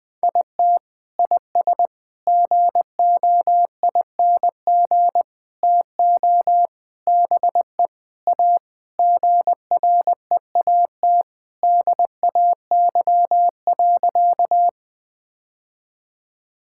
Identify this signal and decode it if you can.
Morse code, heard as a keyed tone, about 20 wpm, 700 Hz. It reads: IT IS GOING TO BE A GREAT DAY.
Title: Sentences with 3-Word Phrases - 20wpm